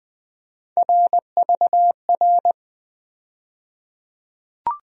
Morse code, heard as a keyed tone, about 20 wpm, 700 Hz. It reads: RVR
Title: USA Call Sign Postfixes - 3 Letters 20wpm